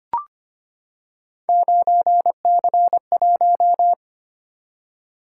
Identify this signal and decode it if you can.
Morse code, heard as a keyed tone, about 25 wpm, 700 Hz. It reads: 9C1
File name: Three Letter-Number Combinations - 25wpm